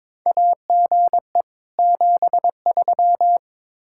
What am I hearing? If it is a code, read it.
Morse code, heard as a keyed tone, about 22 wpm, 700 Hz. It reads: AGE 73
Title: Practice QSO - Exchange 3 - More Info - 22wpm